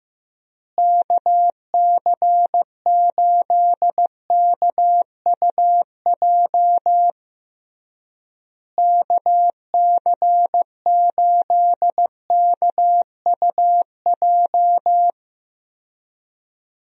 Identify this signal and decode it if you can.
Morse code, heard as a keyed tone, about 15 wpm, 700 Hz. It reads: KC8KUJ KC8KUJ
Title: USA Call Signs - 6 Characters - Repeated 3x - 15wpm